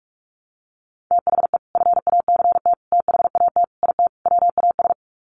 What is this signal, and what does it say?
Morse code, heard as a keyed tone, about 44 wpm, 700 Hz. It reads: THE FACT THAT IT WAS
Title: Sets of 5 Words Encore2 45wpm